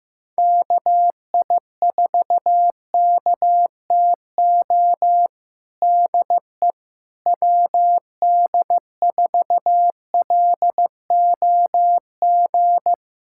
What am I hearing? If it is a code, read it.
Morse code, heard as a keyed tone, about 16 wpm, 700 Hz. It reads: KI4KTO DE WD4LOG
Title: USA Callsign DE Callsign - 6 Characters - 15wpm